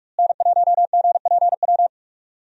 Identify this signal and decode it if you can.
Morse code, heard as a keyed tone, about 46 wpm, 700 Hz. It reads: N1GPW
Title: USA Call Signs - 5 Characters - Repeated 2x - 45wpm